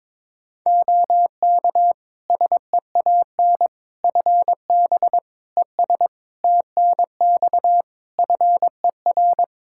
Morse code, heard as a keyed tone, about 22 wpm, 700 Hz. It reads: OK SEAN FB ES TNX FER
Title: Practice QSO - Exchange 2 - Info - 22wpm